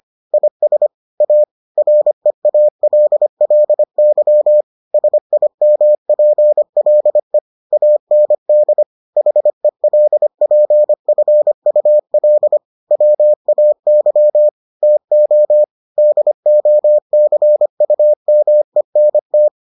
Morse code, heard as a keyed tone, about 25 wpm, 600 Hz. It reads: IS A REALLY SIMPLE AND HELPFUL WAY TO DOCUMENT